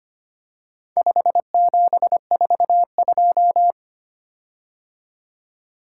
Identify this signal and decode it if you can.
Morse code, heard as a keyed tone, about 25 wpm, 700 Hz. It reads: 5742
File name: practice / 25wpm / Four Letter-Number Combinations - 25wpm